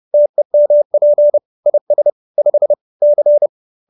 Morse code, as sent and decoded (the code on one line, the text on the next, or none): - . -- .--. / .. ... / ..... / -.-.
TEMP IS 5 C